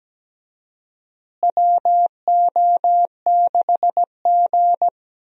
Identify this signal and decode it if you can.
Morse code, heard as a keyed tone, about 17 wpm, 700 Hz. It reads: WO6G
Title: USA Call Signs - 4 Characters - Repeated 3x - 17wpm